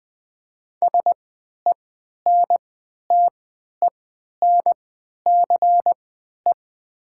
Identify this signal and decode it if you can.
Morse code, heard as a keyed tone, about 20 wpm, spaced as for 11 wpm, 700 Hz. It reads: SENTENCE
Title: 3x Character Spacing - Top 300 Words - 20wpm